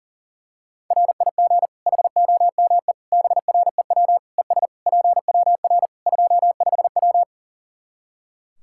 Morse code, heard as a keyed tone, about 40 wpm, 700 Hz. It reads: RIG HOME BREW ES PWR 25W
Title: Practice QSO - Exchange 2 - Info - 40wpm